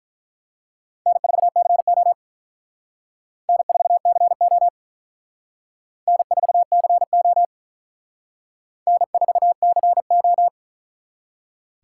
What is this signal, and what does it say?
Morse code, heard as a keyed tone, about 43 wpm, 700 Hz. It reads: N4CO N4CO N4CO N4CO
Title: Speed Racing - USA Call Signs 35wpm